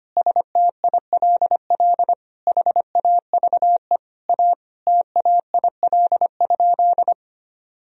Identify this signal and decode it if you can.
Morse code, heard as a keyed tone, about 25 wpm, 700 Hz. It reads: STILL HAVE A TAIL?
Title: Sentences from Top 1000 Words - Review Part 3 - 25wpm